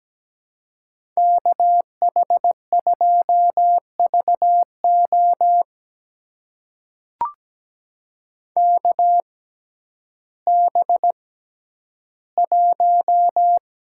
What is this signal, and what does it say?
Morse code, heard as a keyed tone, about 17 wpm, spaced as for 10 wpm, 700 Hz. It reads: KH2VO KB1
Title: USA Call Signs - 6x Character Spacing - 17wpm